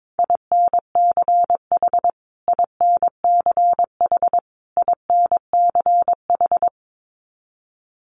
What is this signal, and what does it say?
Morse code, heard as a keyed tone, about 22 wpm, 700 Hz. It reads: INCH INCH INCH
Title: Top 500 Words Repeated 22wpm